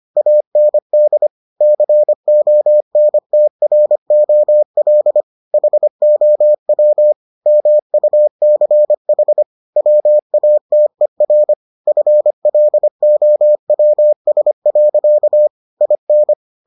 Morse code, heard as a keyed tone, about 25 wpm, 600 Hz. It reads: AND CONTROL HOW MUCH WATER FLOWS. IN